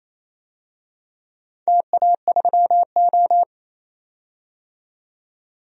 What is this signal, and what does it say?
Morse code, heard as a keyed tone, about 28 wpm, 700 Hz. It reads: TA3O
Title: DXCC-100 Call Signs - 4 Characters - 28wpm